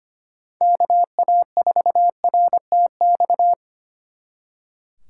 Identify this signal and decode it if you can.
Morse code, heard as a keyed tone, about 33 wpm, 700 Hz. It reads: KA4RTX